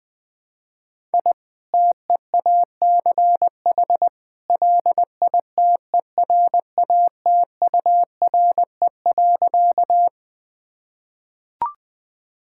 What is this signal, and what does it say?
Morse code, heard as a keyed tone, about 21 wpm, 700 Hz. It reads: I TEACH LITERATURE.
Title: Sentences from Top 1000 Words - Review Part 6 - 20wpm